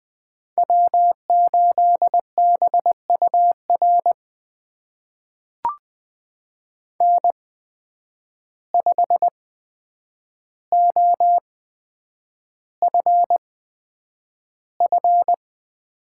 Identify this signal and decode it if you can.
Morse code, heard as a keyed tone, about 20 wpm, spaced as for 9 wpm, 700 Hz. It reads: W8BUR N5OFF
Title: USA Call Signs - 5 Characters - 8x Character Spacing - 20wpm